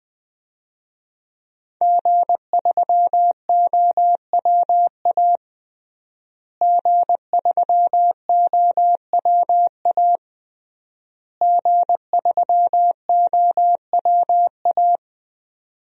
Morse code, heard as a keyed tone, about 20 wpm, 700 Hz. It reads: G3OWA G3OWA G3OWA
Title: DXCC-100 Call Signs - 5 Characters - Repeated 3x - 20wpm